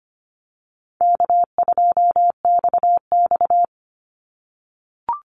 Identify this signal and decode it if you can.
Morse code, heard as a keyed tone, about 25 wpm, 700 Hz. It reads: K2XX
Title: Easy Call Signs - 25wpm